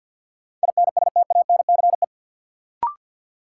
Morse code, heard as a keyed tone, about 51 wpm, 700 Hz. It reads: INSTANCE
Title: Sentences from Top 800 Words - New Word Drill - 50wpm